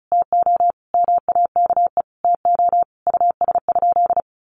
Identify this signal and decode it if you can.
Morse code, heard as a keyed tone, about 34 wpm, 700 Hz. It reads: TO MAKE TO US?
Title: Sentences from Top 1400 Words 35wpm